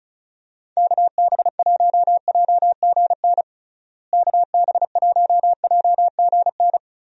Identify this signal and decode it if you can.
Morse code, heard as a keyed tone, about 36 wpm, 700 Hz. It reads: KB1JGN KB1JGN
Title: USA Call Signs - 6 Characters - Repeated 3x - 35wpm